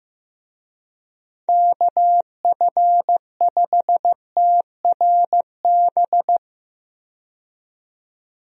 Morse code, heard as a keyed tone, about 15 wpm, 700 Hz. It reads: KF5TRB
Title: USA Call Signs - 6 Characters - 15wpm